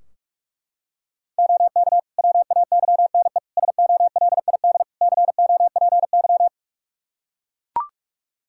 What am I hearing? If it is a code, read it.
Morse code, heard as a keyed tone, about 41 wpm, 700 Hz. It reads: OK WAYNE SOLID COPY
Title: Practice QSO - Exchange 3 - More Info - 45wpm